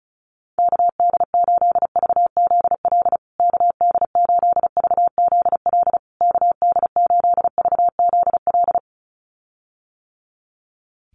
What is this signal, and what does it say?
Morse code, heard as a keyed tone, about 34 wpm, 700 Hz. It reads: KD8VZL KD8VZL KD8VZL
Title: USA Call Signs Repeated 3x 35wpm